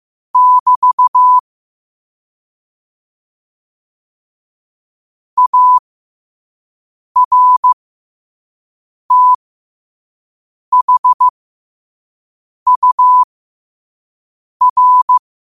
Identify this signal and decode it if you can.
Morse code, heard as a keyed tone, about 15 wpm, spaced as for 5 wpm, 1000 Hz. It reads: <BT> ARTHUR
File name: qotdfiles / 5wpm